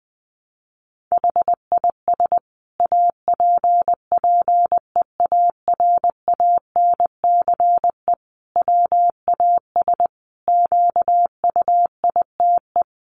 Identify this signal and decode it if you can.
Morse code, heard as a keyed tone, about 20 wpm, 700 Hz. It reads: HIS APPEARANCE WAS QUITE
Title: Sentences from Top 1100 Words 20wpm